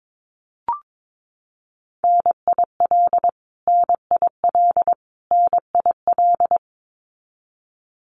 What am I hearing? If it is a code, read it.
Morse code, heard as a keyed tone, about 22 wpm, 700 Hz. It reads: NIL NIL NIL
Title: Abbreviations Repeated 3x - 22wpm